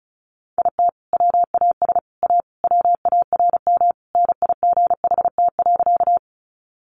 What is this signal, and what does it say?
Morse code, heard as a keyed tone, about 34 wpm, 700 Hz. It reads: IT WAS A WARM NIGHT.
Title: Sentences from Top 500 Words 35wpm